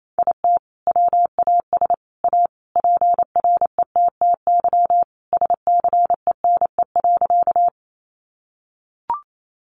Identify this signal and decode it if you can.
Morse code, heard as a keyed tone, about 28 wpm, 700 Hz. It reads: IT WAS A PRETTY SCENE.
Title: Sentences from Top 1100 Words 28wpm